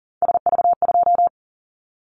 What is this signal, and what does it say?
Morse code, heard as a keyed tone, about 38 wpm, 700 Hz. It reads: SV2
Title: Callsign Prefixes for Top 50 DXCC - 40wpm